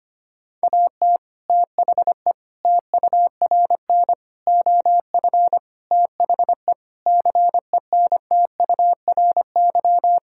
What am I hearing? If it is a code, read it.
Morse code, heard as a keyed tone, about 25 wpm, 700 Hz. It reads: AT THE TURN OF THE CENTURY
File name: practice / 25wpm / Sentences with 4-Word Phrases - 25wpm